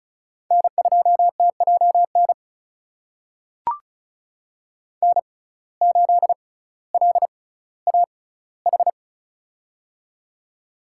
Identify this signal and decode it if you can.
Morse code, heard as a keyed tone, about 36 wpm, spaced as for 18 wpm, 700 Hz. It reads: N2TJN N8LAH
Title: USA Call Signs - 6x Character Spacing - 35wpm